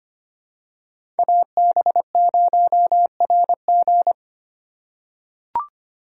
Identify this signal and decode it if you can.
Morse code, heard as a keyed tone, about 25 wpm, 700 Hz. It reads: AB0RG
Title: USA Call Signs - 5 Characters - 2x Character Spacing - 25wpm